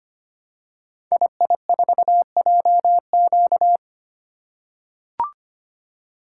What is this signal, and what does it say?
Morse code, heard as a keyed tone, about 25 wpm, 700 Hz. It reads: II4JQ
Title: Code Groups - 5 Letter:Numbers - 25wpm